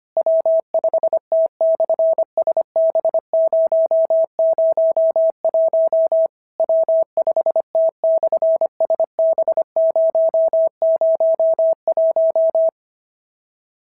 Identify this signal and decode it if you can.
Morse code, heard as a keyed tone, about 25 wpm, 650 Hz. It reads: W5T/SB001 W5T/SB001
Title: SOTA QSO Practice 25wpm